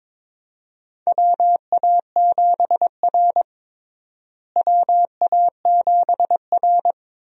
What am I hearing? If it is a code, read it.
Morse code, heard as a keyed tone, about 22 wpm, 700 Hz. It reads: WA7R WA7R
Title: USA Call Signs - 4 Characters - Repeated 2x - 22wpm